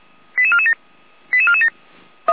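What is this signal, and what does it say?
Five-tone selective call (ZVEI), heard as quick runs of five tones, tones of about 70 ms. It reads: 88447 88447
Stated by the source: ZVEI five-tone selective call 88352, received via SDR.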